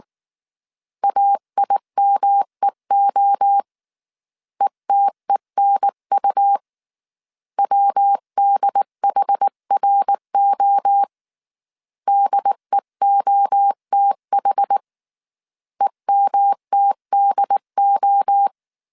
Morse code, heard as a keyed tone, about 19 wpm, 800 Hz. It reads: AIMEO ETENU WDHRO DEOTH EMTDO